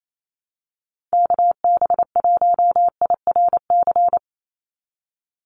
Morse code, heard as a keyed tone, about 28 wpm, 700 Hz. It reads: KB1IRC
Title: USA Call Signs 28wpm